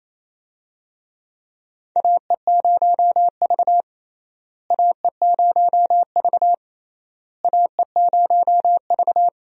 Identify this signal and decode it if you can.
Morse code, heard as a keyed tone, about 28 wpm, 700 Hz. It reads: AE0V AE0V AE0V
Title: USA Call Signs - 4 Characters - Repeated 3x - 28wpm